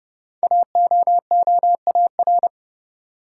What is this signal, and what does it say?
Morse code, heard as a keyed tone, about 29 wpm, 700 Hz. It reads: AOOAR
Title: Code Groups - 5 Letter:Numbers - 30wpm